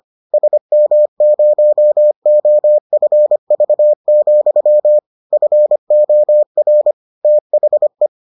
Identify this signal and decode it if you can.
Morse code, heard as a keyed tone, about 25 wpm, 600 Hz. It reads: SM0OFV, FOR THE